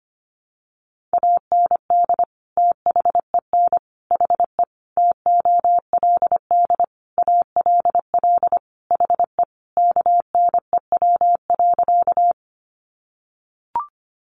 Morse code, heard as a keyed tone, about 25 wpm, 700 Hz. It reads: AND THEN HE TOLD ALL HE KNEW.
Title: Sentences from Top 400 Words 25wpm